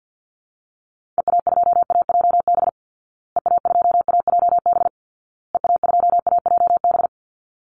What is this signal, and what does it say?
Morse code, heard as a keyed tone, about 48 wpm, 700 Hz. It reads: EA2AJB EA2AJB EA2AJB
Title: Call Signs Repeated 2x - 50wpm